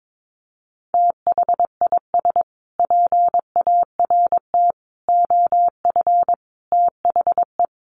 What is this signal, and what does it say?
Morse code, heard as a keyed tone, about 22 wpm, 700 Hz. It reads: THIS PART OF THE
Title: Sets of 5 Words Encore2 22wpm